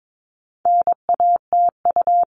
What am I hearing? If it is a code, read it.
Morse code, heard as a keyed tone, about 22 wpm, 700 Hz. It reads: NATU
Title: All Cut Numbers - 4-Digit 22wpm